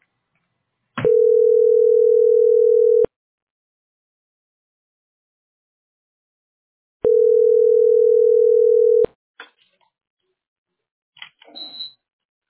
Ringback tone (North America), 440 + 480 Hz.